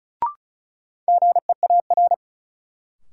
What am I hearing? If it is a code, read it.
Morse code, heard as a keyed tone, about 36 wpm, 700 Hz. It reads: GEAR
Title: Sentences from Top 1400 Words - New Word Drill - 35wpm